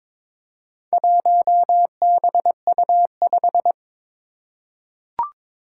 Morse code, heard as a keyed tone, about 22 wpm, 700 Hz. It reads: 1BU5
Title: Four Challenging Letter-Number Combinations 22wpm